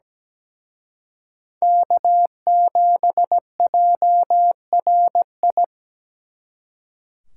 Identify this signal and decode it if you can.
Morse code, heard as a keyed tone, about 20 wpm, 700 Hz. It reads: K7JRI
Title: Speed Racing - USA 1x3 Call Signs 17wpm